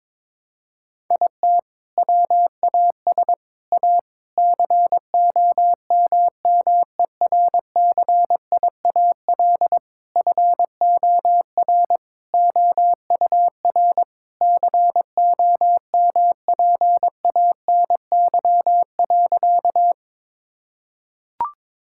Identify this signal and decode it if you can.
Morse code, heard as a keyed tone, about 22 wpm, 700 Hz. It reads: IT WAS A COMMERCIAL FOR OUR COMPANY.
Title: Sentences from Top 1000 Words - Review Part 5 - 22wpm